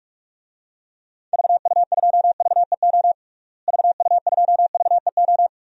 Morse code, heard as a keyed tone, about 47 wpm, 700 Hz. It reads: VU2VEO VU2VEO
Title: DXCC-100 Call Signs - 6 Characters - Repeated 2x - 45wpm